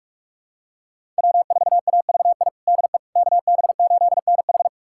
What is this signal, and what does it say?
Morse code, heard as a keyed tone, about 46 wpm, 700 Hz. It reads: W4AVI DE KB8NH
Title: USA Callsign DE Callsign - 5 Characters - 45wpm